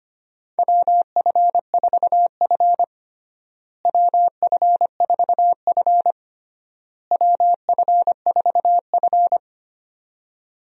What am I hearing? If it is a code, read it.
Morse code, heard as a keyed tone, about 25 wpm, 700 Hz. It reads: WF4F WF4F WF4F
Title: USA Call Signs - 4 Characters - Repeated 3x - 25wpm